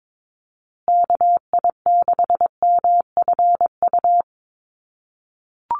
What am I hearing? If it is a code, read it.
Morse code, heard as a keyed tone, about 22 wpm, 700 Hz. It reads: KI6MFU
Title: USA Call Signs Repeated 3x 22wpm